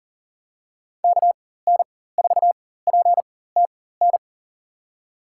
Morse code, heard as a keyed tone, about 40 wpm, spaced as for 19 wpm, 700 Hz. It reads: KN4PTN
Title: USA Call Signs - 4x Character Spacing - 40wpm